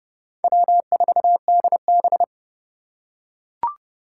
Morse code, heard as a keyed tone, about 29 wpm, 700 Hz. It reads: W4DB